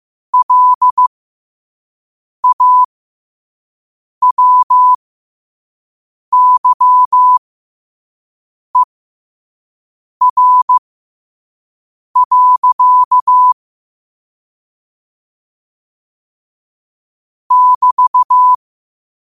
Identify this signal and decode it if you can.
Morse code, heard as a keyed tone, about 15 wpm, spaced as for 5 wpm, 1000 Hz. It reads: LAWYER. <BT>
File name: qotdfiles / 5wpm